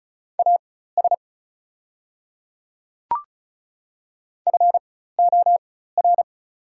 Morse code, heard as a keyed tone, about 36 wpm, spaced as for 16 wpm, 700 Hz. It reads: AS FOR
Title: 4x Character Spacing - Top 100 Words - 35wpm